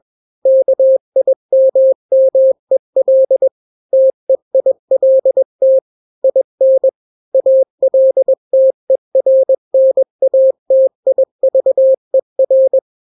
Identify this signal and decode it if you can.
Morse code, heard as a keyed tone, about 21 wpm, 530 Hz. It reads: KIMMEL TEILT IN ALTERNATIVER
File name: daily / 20wpm